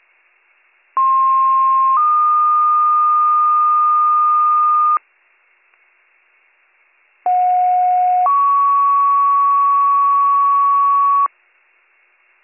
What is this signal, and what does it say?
Two-tone page, heard as a one-second tone, then a three-second tone, about 1060 then 1220 Hz, about 730 then 1120 Hz. It is heard from 1.0 to 11.3 s.